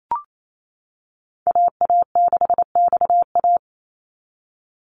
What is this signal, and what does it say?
Morse code, heard as a keyed tone, about 28 wpm, 700 Hz. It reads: AA6XA
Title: NA SOTA Activator Callsigns 28wpm